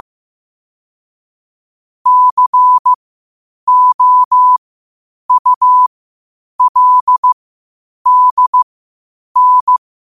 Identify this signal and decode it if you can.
Morse code, heard as a keyed tone, about 15 wpm, spaced as for 8 wpm, 1000 Hz. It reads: COULDN
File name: qotdfiles / 8wpm